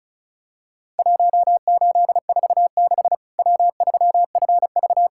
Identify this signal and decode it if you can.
Morse code, heard as a keyed tone, about 36 wpm, 700 Hz. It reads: 1846 W3FV
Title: Worked All Europe DX Contest - QTC Exchange - 35wpm